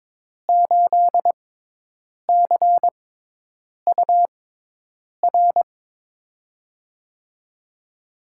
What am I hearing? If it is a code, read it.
Morse code, heard as a keyed tone, about 22 wpm, spaced as for 8 wpm, 700 Hz. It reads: 8CUR
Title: USA Call Signs - 6 Characters - 6x Character Spacing - 22wpm